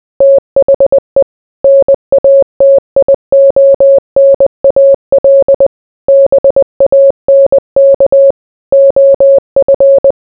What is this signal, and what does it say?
Morse code, heard as a keyed tone, about 20 wpm, 550 Hz. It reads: THE NATIONAL BANK OF